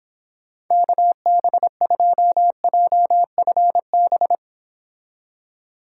Morse code, heard as a keyed tone, about 30 wpm, 700 Hz. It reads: KB2JFB